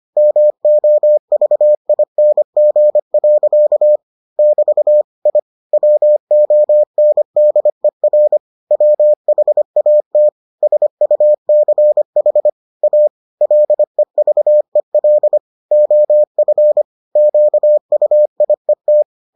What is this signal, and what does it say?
Morse code, heard as a keyed tone, about 25 wpm, 600 Hz. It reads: MOVING. <BT> I WONDER WHAT SUCH A LEVEL OF QUIET